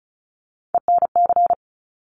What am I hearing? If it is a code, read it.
Morse code, heard as a keyed tone, about 36 wpm, 700 Hz. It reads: ENC
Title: Top Three Letter Combinations in Words 35wpm